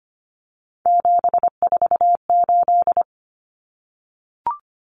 Morse code, heard as a keyed tone, about 25 wpm, 700 Hz. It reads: 748
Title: Common Cut Numbers - 3-Digit 25wpm